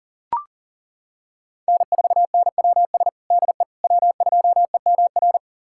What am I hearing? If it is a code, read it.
Morse code, heard as a keyed tone, about 37 wpm, 700 Hz. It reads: N4NWS DE W2EMR
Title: USA Callsign DE Callsign - 5 Characters - 40wpm